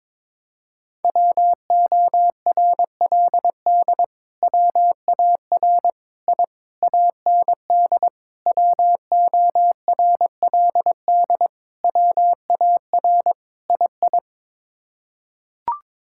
Morse code, heard as a keyed tone, about 22 wpm, 700 Hz. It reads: WORLD WAR I AND WORLD WAR II
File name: practice / 22wpm / Sets of 7 Words - 22wpm